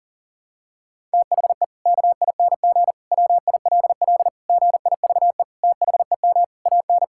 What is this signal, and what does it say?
Morse code, heard as a keyed tone, about 40 wpm, 700 Hz. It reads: THE KING WILL GIVE THEM AN
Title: Sentences from Top 1000 Words - Review Part 5 - 40wpm